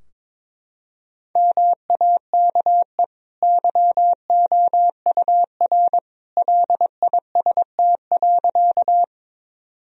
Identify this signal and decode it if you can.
Morse code, heard as a keyed tone, about 21 wpm, 700 Hz. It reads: MAKE YOUR LIST.
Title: Sentences from Top 1000 Words - Review Part 2 - 22wpm